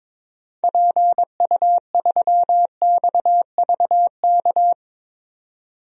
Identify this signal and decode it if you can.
Morse code, heard as a keyed tone, about 22 wpm, 700 Hz. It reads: PU3XVK
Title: DXCC-100 Call Signs - 6 Characters - 22wpm